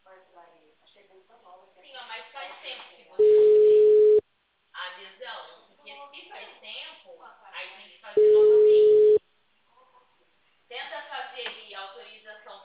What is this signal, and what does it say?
Ringback tone (Europe, 425 Hz), with speech.
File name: out- -2004- - - .47464